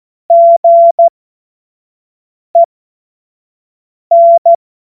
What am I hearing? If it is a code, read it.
Morse code, heard as a keyed tone, about 14 wpm, spaced as for 5 wpm, 680 Hz. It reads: GEN
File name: Morsekurs / Witze / Witz 16 Januar 2024 25 F70